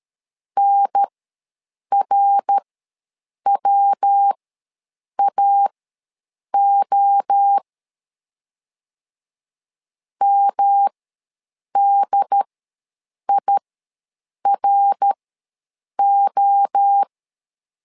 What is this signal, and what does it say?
Morse code, heard as a keyed tone, about 13 wpm, spaced as for 7 wpm, 800 Hz. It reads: NRWAO MDIRO
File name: LEZIONI / 3 MWRO / 3 MWRO ws 12wpm 1914